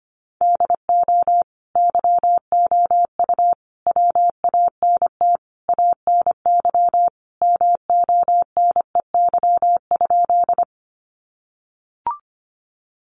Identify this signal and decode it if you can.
Morse code, heard as a keyed tone, about 25 wpm, 700 Hz. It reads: DO YOU WANT ANY MONEY?
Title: Sentences from Top 400 Words 25wpm